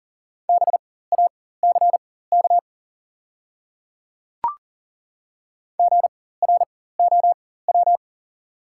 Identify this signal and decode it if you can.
Morse code, heard as a keyed tone, about 40 wpm, spaced as for 19 wpm, 700 Hz. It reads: BACK GROW
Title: 4x Character Spacing - Top 300 Words - 40wpm